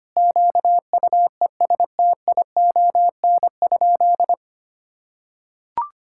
Morse code, heard as a keyed tone, about 25 wpm, 700 Hz. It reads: QUESTION?
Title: Sentences from Top 1000 Words - Review Part 5 - 25wpm